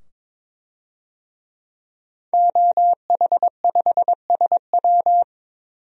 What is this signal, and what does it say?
Morse code, heard as a keyed tone, about 22 wpm, 700 Hz. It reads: OH5SW
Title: DXCC-100 Call Signs - 5 Characters - 22wpm